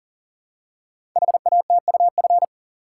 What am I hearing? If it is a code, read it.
Morse code, heard as a keyed tone, about 42 wpm, 690 Hz. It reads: HATUF
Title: Code Groups - 5 Letters - 40wpm